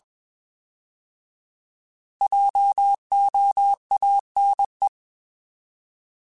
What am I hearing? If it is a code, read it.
Morse code, heard as a keyed tone, about 21 wpm, 800 Hz. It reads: JOANE
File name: Alphabetical / Alphabetical mp3s / Js @20